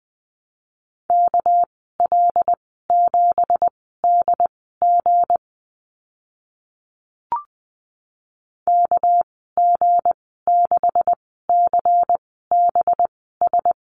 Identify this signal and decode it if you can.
Morse code, heard as a keyed tone, about 20 wpm, spaced as for 14 wpm, 700 Hz. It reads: KL7DG KG6CBS
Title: 207 Lesson 33g - Letter j - Call Signs 20wpm